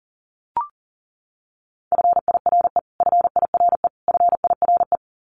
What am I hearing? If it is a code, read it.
Morse code, heard as a keyed tone, about 40 wpm, 700 Hz. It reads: FIRE FIRE FIRE
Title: Top 500 Words Repeated 40wpm